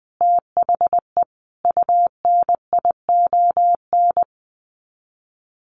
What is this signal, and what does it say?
Morse code, heard as a keyed tone, about 20 wpm, 700 Hz. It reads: THE UNION
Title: Sets of 5 Words Encore 20wpm